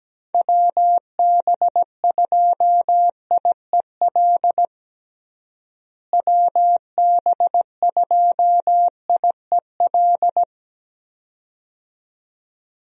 Morse code, heard as a keyed tone, about 17 wpm, 700 Hz. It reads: WB2IEL WB2IEL
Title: USA Call Signs - 6 Characters - Repeated 2x - 17wpm